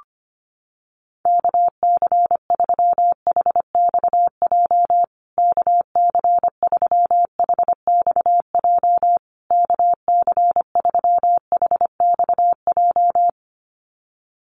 Morse code, heard as a keyed tone, about 25 wpm, 700 Hz. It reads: KC3HXJ KC3HXJ KC3HXJ
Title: USA Call Signs Repeated 3x 25wpm